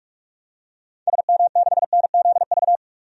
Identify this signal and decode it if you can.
Morse code, heard as a keyed tone, about 42 wpm, 700 Hz. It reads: SM6NZV
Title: DXCC-100 Call Signs - 6 Characters - 45wpm